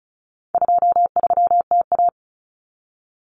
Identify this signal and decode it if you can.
Morse code, heard as a keyed tone, about 34 wpm, 700 Hz. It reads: 23TA
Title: Common Cut Numbers - 4-Digit 35wpm